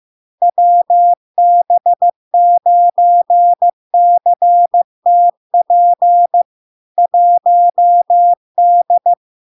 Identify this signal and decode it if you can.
Morse code, heard as a keyed tone, about 15 wpm, 700 Hz. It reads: WB9CTP 1D
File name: LARK-FD-2022-Log-in-CW-15wpm-700-Hz